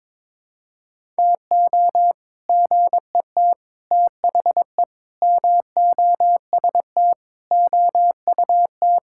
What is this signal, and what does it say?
Morse code, heard as a keyed tone, about 22 wpm, 700 Hz. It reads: TO GET THE MOST OUT
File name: Sentences with 7-Word Phrases - 22wpm